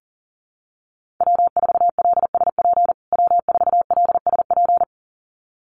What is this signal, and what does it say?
Morse code, heard as a keyed tone, about 40 wpm, 700 Hz. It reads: W4LSP W4LSP
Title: USA Call Signs Repeated 2x 40wpm